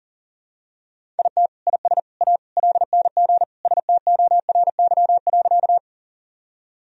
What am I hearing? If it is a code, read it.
Morse code, heard as a keyed tone, about 40 wpm, 700 Hz. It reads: IT IS A LNG STORY.
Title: Sentences from Top 300 Words and Abbreviations 40wpm